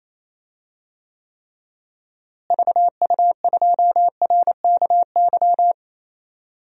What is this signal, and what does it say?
Morse code, heard as a keyed tone, about 28 wpm, 700 Hz. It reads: VU2RKY
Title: DXCC-100 Call Signs - 6 Characters - Repeated 3x - 28wpm